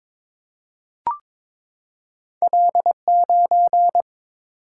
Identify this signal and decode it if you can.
Morse code, heard as a keyed tone, about 22 wpm, 700 Hz. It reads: L9
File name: practice / 22wpm / Two Letter-Number Combinations - 22wpm